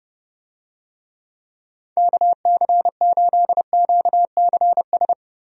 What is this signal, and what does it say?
Morse code, heard as a keyed tone, about 30 wpm, 700 Hz. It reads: KC8QCS
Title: USA Call Signs - 6 Characters - Repeated 3x - 30wpm